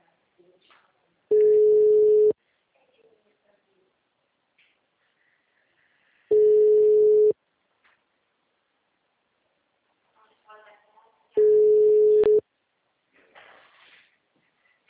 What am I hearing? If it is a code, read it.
Ringback tone (Europe, 425 Hz), with speech.
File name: out- -2014- - - .38333